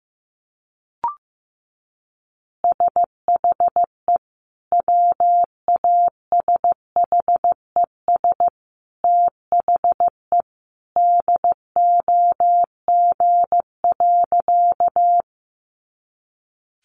Morse code, heard as a keyed tone, about 15 wpm, 700 Hz. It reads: SHE WASHES THE DOG.
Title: Easy Phrases 15wpm